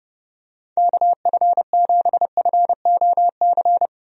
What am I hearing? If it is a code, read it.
Morse code, heard as a keyed tone, about 30 wpm, 700 Hz. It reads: KF7FOC
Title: USA Call Signs - 6 Characters - Repeated 2x - 30wpm